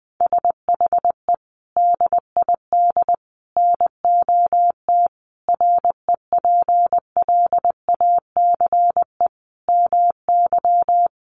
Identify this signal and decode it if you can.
Morse code, heard as a keyed tone, about 20 wpm, 700 Hz. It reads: SHE DID NOT REPLACE MY
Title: Sentences from Top 1400 Words 20wpm